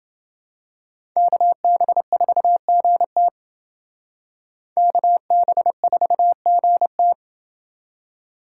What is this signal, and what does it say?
Morse code, heard as a keyed tone, about 28 wpm, 700 Hz. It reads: KB4GT KB4GT